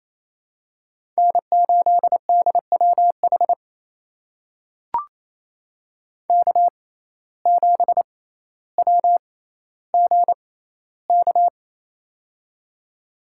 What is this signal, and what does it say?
Morse code, heard as a keyed tone, about 28 wpm, spaced as for 14 wpm, 700 Hz. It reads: N8DWH K7WGK
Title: USA Call Signs - 5 Characters - 6x Character Spacing - 28wpm